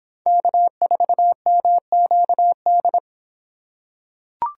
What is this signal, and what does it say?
Morse code, heard as a keyed tone, about 26 wpm, 700 Hz. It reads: K4MQD